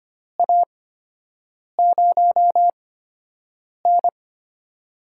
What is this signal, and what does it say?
Morse code, heard as a keyed tone, about 25 wpm, spaced as for 7 wpm, 700 Hz. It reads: A0N
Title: USA Call Signs - 4 Characters - 8x Character Spacing - 25wpm